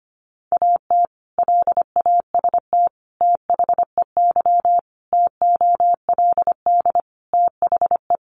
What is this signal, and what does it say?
Morse code, heard as a keyed tone, about 25 wpm, 700 Hz. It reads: AT LAST THEY TOLD THE
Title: Sentences from Top 1200 Words 25wpm